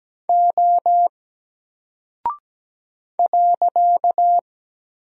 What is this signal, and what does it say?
Morse code, heard as a keyed tone, about 17 wpm, 700 Hz. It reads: O .
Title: Single Character (Letter:Number:Punctuation:Prosign) 17wpm